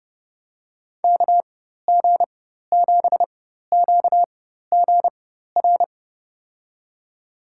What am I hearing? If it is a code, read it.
Morse code, heard as a keyed tone, about 30 wpm, spaced as for 14 wpm, 700 Hz. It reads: KG7QGR
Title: USA Call Signs - 4x Character Spacing - 30wpm